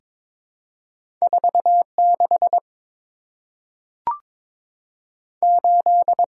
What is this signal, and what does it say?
Morse code, heard as a keyed tone, about 22 wpm, 700 Hz. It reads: 4 6 8
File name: practice / 22wpm / Two Letter-Number Combinations - 22wpm